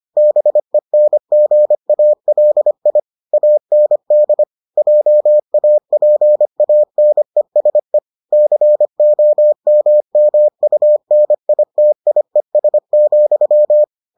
Morse code, heard as a keyed tone, about 25 wpm, 600 Hz. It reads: BENGALI AND JAPANESE COMMUNITIES,